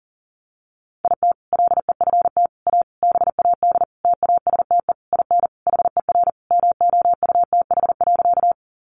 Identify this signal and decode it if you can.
Morse code, heard as a keyed tone, about 40 wpm, 700 Hz. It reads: IT LEFT A BAD TASTE IN HER MOUTH.
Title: Sentences from Top 1500 Words 40wpm